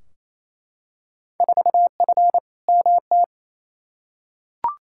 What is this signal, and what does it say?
Morse code, heard as a keyed tone, about 28 wpm, 700 Hz. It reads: 4F MT
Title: Field Day Exchanges Encore - 28wpm